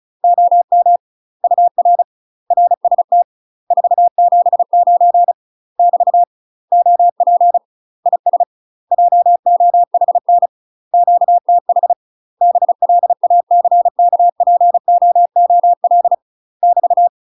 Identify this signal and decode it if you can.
Morse code, heard as a keyed tone, about 36 wpm, 700 Hz. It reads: OM UR RST 479 <BT> OP IS JOHN QTH BLACKPOOL <BT>
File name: UK MASTER QSOs TRAINING / 35WPM / UK MASTER QSOs 0001